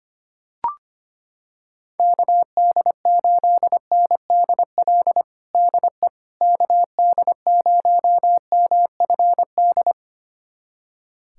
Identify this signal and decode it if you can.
Morse code, heard as a keyed tone, about 25 wpm, 700 Hz. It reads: KD8NDL DE KD0MFD
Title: USA Callsign DE Callsign 25wpm